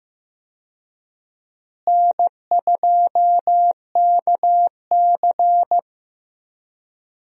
Morse code, heard as a keyed tone, about 15 wpm, 700 Hz. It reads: N2KC
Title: USA Call Signs - 4 Characters - Repeated 3x - 15wpm